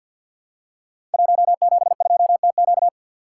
Morse code, heard as a keyed tone, about 52 wpm, 700 Hz. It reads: 172TQ
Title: Code Groups - 5 Letter:Numbers - 50wpm